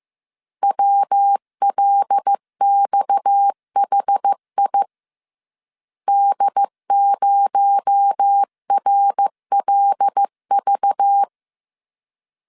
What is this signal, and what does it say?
Morse code, heard as a keyed tone, about 15 wpm, 800 Hz. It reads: WLXHI D0RLV
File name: CW / DET / 1 6 1500C S C 14wpm 2951